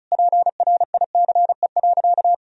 Morse code, heard as a keyed tone, about 36 wpm, 700 Hz. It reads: PRICE.
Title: Sentences from Top 1000 Words - Review Part 4 - 35wpm